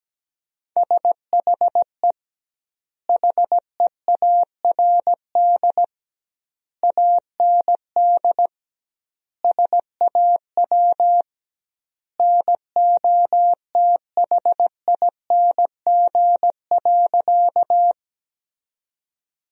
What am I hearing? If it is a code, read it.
Morse code, heard as a keyed tone, about 17 wpm, 700 Hz. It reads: SHE HEARD AND SAW NOTHING.
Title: 2x Word-Spacing - Sentences from Top 500 Words - 17wpm